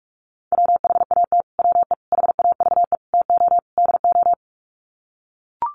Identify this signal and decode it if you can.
Morse code, heard as a keyed tone, about 44 wpm, 700 Hz. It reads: WHAT WE HAVE TO DO
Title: Sets of 5 Words Encore 45wpm